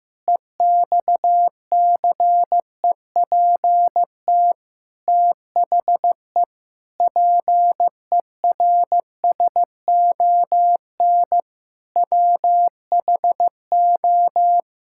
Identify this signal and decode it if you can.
Morse code, heard as a keyed tone, about 15 wpm, 700 Hz. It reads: EXCEPT THE PERSON WHO
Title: Sentences from Top 1600 Words 15wpm